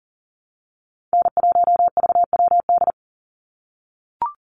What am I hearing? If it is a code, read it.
Morse code, heard as a keyed tone, about 40 wpm, 700 Hz. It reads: N1VWD
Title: NA SOTA Chaser Callsigns 40wpm